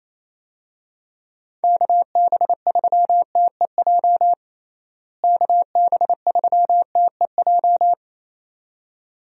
Morse code, heard as a keyed tone, about 28 wpm, 700 Hz. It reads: KB3TEJ KB3TEJ
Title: USA Call Signs - 6 Characters - Repeated 2x - 28wpm